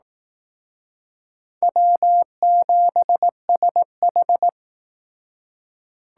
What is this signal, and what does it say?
Morse code, heard as a keyed tone, about 19 wpm, 700 Hz. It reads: W7SH
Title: Speed Racing - USA 1x2 Call Signs 15wpm